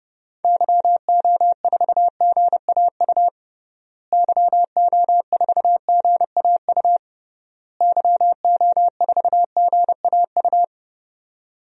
Morse code, heard as a keyed tone, about 32 wpm, 700 Hz. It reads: YO4GAU YO4GAU YO4GAU
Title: DXCC-100 Call Signs - 6 Characters - Repeated 3x - 30wpm